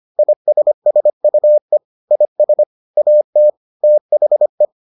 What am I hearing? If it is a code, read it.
Morse code, heard as a keyed tone, about 25 wpm, 600 Hz. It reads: ISSUE IS AT THE